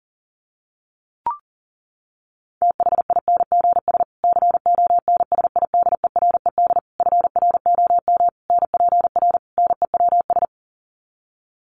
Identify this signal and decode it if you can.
Morse code, heard as a keyed tone, about 40 wpm, 700 Hz. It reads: THINGS CONSIDERED FROM NPR NEWS
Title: Sets of 5 Words Encore 40wpm